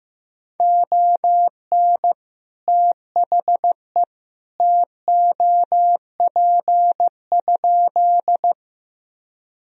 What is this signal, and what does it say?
Morse code, heard as a keyed tone, about 15 wpm, 700 Hz. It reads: ON THE TOP?
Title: Sentences from Top 1000 Words - Review Part 2 - 15wpm